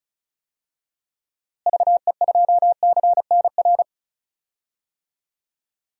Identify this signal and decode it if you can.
Morse code, heard as a keyed tone, about 34 wpm, 700 Hz. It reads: VE2CNR
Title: DXCC-100 Call Signs - 6 Characters - Repeated 3x - 35wpm